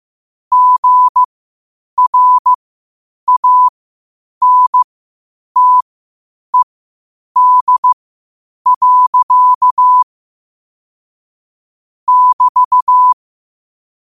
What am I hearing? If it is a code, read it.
Morse code, heard as a keyed tone, about 15 wpm, spaced as for 8 wpm, 1000 Hz. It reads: GRANTED. <BT>
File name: qotdfiles / 8wpm